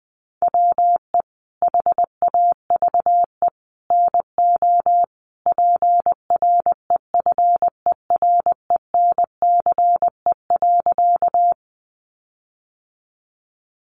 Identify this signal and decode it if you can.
Morse code, heard as a keyed tone, about 20 wpm, 700 Hz. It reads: WE HAVE NO PREFERENCE.
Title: Sentences from Top 1500 Words 20wpm